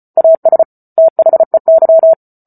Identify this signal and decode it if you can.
Morse code, heard as a keyed tone, about 34 wpm, 650 Hz. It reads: AS THEY
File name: Book Two / 08-Chapter 8 - 34 WPM